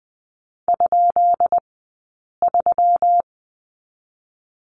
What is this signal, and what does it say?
Morse code, heard as a keyed tone, about 20 wpm, spaced as for 8 wpm, 700 Hz. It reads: ?3
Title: 171 Lesson 28d - Number 9 - Two Characters 20wpm